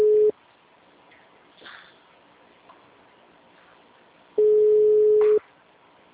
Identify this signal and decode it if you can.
Ringback tone (Europe, 425 Hz).
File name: out- -2003- - - .2765